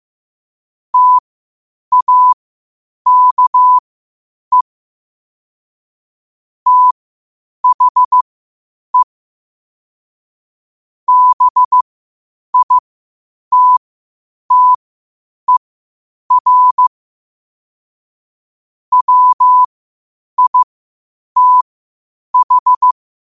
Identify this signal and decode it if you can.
Morse code, heard as a keyed tone, about 15 wpm, spaced as for 8 wpm, 1000 Hz. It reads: TAKE THE BITTER WITH
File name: qotdfiles / 8wpm